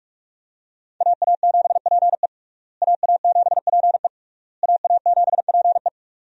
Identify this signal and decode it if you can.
Morse code, heard as a keyed tone, about 46 wpm, 700 Hz. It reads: AA7PE AA7PE AA7PE
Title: USA Call Signs - 5 Characters - Repeated 3x - 45wpm